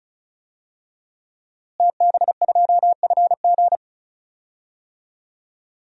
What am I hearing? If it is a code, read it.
Morse code, heard as a keyed tone, about 36 wpm, 690 Hz. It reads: TB2FG
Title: DXCC-100 Call Signs - 5 Characters - Repeated 3x - 35wpm